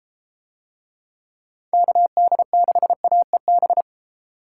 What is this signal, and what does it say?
Morse code, heard as a keyed tone, about 30 wpm, 700 Hz. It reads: KD6AEB